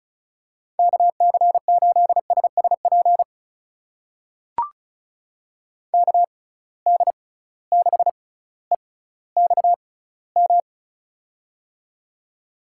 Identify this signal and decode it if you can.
Morse code, heard as a keyed tone, about 36 wpm, spaced as for 18 wpm, 700 Hz. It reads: KC8SSP KD6EXM
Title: USA Call Signs - 6 Characters - 6x Character Spacing - 35wpm